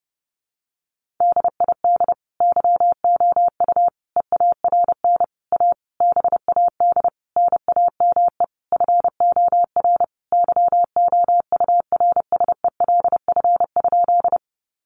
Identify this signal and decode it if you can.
Morse code, heard as a keyed tone, about 30 wpm, 700 Hz. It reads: DID YOU EARN A BAD NAME FOR YOURSELF?
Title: Sentences from Top 1500 Words 30wpm